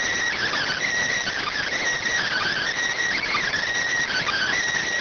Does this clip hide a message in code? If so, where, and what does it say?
Yes: Emergency Alert System (SAME) tones (header bursts) from 0.4 to 4.6 s, heard as harsh buzzing bursts.